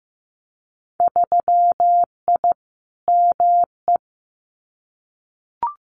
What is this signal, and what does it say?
Morse code, heard as a keyed tone, about 15 wpm, 700 Hz. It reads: 3I ME
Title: Winter Field Day Exchanges 15wpm